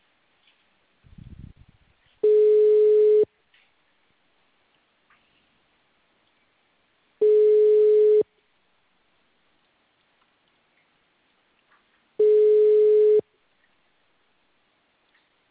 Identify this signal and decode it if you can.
Ringback tone (Europe, 425 Hz).
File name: out- -2013- - - .95100